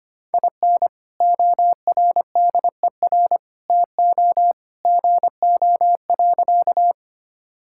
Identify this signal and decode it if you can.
Morse code, heard as a keyed tone, about 25 wpm, 700 Hz. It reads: IN ORDER TO GO.
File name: practice / 25wpm / Sentences with 2-Word Phrases - 25wpm